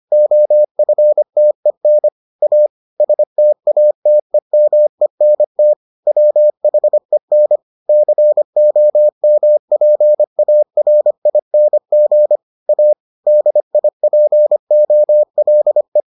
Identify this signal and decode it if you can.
Morse code, heard as a keyed tone, about 25 wpm, 600 Hz. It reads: OFTEN A STATEMENT WHEN COMPARING A DIPOLE